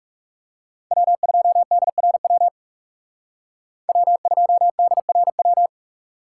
Morse code, heard as a keyed tone, about 35 wpm, 700 Hz. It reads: W2DRW W2DRW
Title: Speed Racing - USA 1x3 Call Signs 30wpm